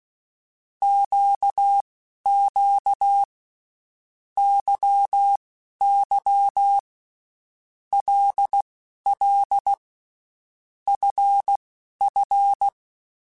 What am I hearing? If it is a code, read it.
Morse code, heard as a keyed tone, about 16 wpm, spaced as for 11 wpm, 800 Hz. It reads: QQ YY LL FF
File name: Groups@10 / Groups@10 mp3s / LFCYQ@10